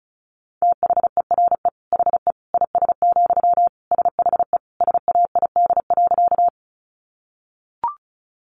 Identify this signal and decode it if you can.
Morse code, heard as a keyed tone, about 34 wpm, 700 Hz. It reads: THERE HE IS, SHE SAID.
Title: Sentences from Top 100 Words 35wpm